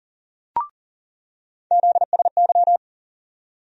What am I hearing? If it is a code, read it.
Morse code, heard as a keyed tone, about 40 wpm, 700 Hz. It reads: ZSY
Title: USA Call Sign Postfixes - 3 Letters 40wpm